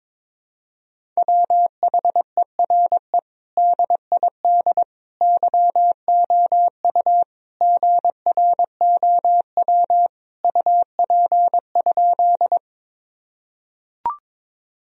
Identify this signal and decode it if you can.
Morse code, heard as a keyed tone, about 22 wpm, 700 Hz. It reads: WHERE DID YOU GROW UP?
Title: Sentences from Top 1000 Words - Review Part 1 - 22wpm